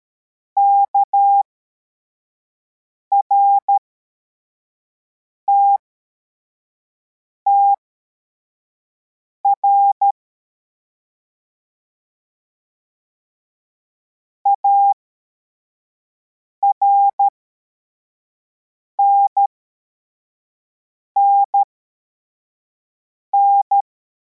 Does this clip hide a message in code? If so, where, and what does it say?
Yes: Morse code from 0.6 to 23.8 s, heard as a keyed tone, about 13 wpm, spaced as for 4 wpm, 800 Hz. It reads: KRTTR ARNNN